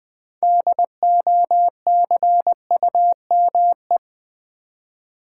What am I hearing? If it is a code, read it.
Morse code, heard as a keyed tone, about 20 wpm, 700 Hz. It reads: DOCUME
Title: Build A Word - 3 Letters At A Time - 8-Letter Words - 20wpm